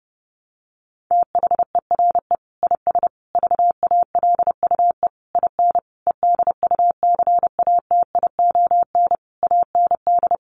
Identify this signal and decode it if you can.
Morse code, heard as a keyed tone, about 30 wpm, 700 Hz. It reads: THERE IS VALUE IN EDUCATION AND
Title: Sentences from Top 1100 Words 30wpm